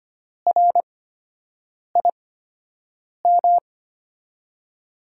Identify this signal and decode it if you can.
Morse code, heard as a keyed tone, about 25 wpm, spaced as for 7 wpm, 700 Hz. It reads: RIM
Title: USA Call Signs - 8x Character Spacing - 25wpm